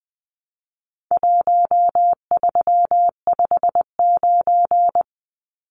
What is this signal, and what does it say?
Morse code, heard as a keyed tone, about 20 wpm, 700 Hz. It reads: 1359
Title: Numbers - Contest Style - 4-Digit 20wpm